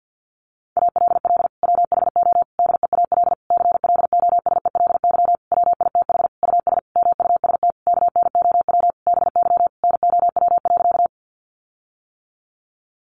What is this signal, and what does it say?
Morse code, heard as a keyed tone, about 50 wpm, 700 Hz. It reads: ALL WHO DEAL CLOSELY WITH US MUST KNOW BY NOW.
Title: Sentences from Top 1400 Words 50wpm